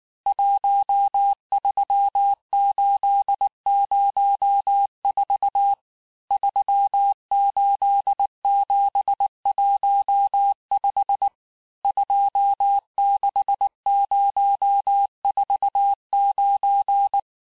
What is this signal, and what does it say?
Morse code, heard as a keyed tone, about 19 wpm, 800 Hz. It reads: 13804 38715 26049
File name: cours-du-lundi files / cours-1583--texte-chiffre-18-mpm